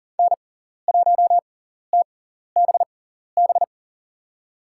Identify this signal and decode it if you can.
Morse code, heard as a keyed tone, about 41 wpm, spaced as for 14 wpm, 700 Hz. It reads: N1TBB
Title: USA Call Signs - 5 Characters - 6x Character Spacing - 40wpm